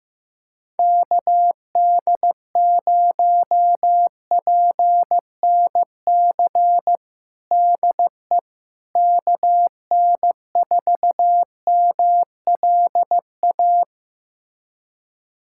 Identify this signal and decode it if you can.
Morse code, heard as a keyed tone, about 15 wpm, 700 Hz. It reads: KD0PNC DE KN4MLA
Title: USA Callsign DE Callsign - 6 Characters - 15wpm